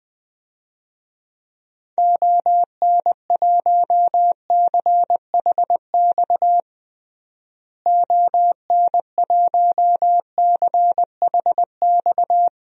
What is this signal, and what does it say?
Morse code, heard as a keyed tone, about 21 wpm, 700 Hz. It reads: ON1CHX ON1CHX
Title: DXCC-100 Call Signs - 6 Characters - Repeated 2x - 20wpm